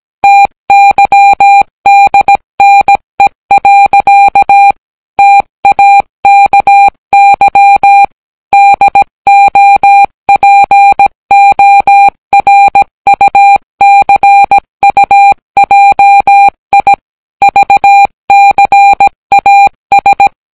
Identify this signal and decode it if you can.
Morse code, heard as a keyed tone, about 17 wpm, 800 Hz. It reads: TYDNE. TAKY DOPORUCUJI VCAS